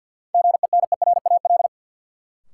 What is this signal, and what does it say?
Morse code, heard as a keyed tone, about 53 wpm, 700 Hz. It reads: GENERAL
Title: Sentences from Top 600 Words - New Word Drill - 50wpm